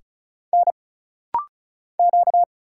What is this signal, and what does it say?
Morse code, heard as a keyed tone, about 35 wpm, 700 Hz. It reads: N Q
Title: Single Letter-Number Alt - Rapid-Fire 35wpm